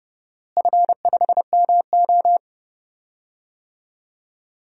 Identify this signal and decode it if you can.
Morse code, heard as a keyed tone, about 30 wpm, 700 Hz. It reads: F5MO
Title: DXCC-100 Call Signs - 4 Characters - Repeated 2x - 30wpm